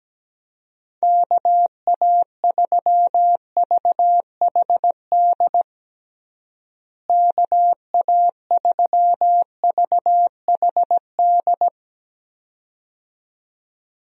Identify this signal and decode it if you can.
Morse code, heard as a keyed tone, about 17 wpm, 700 Hz. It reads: KA3VHD KA3VHD
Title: USA Call Signs - 6 Characters - Repeated 2x - 17wpm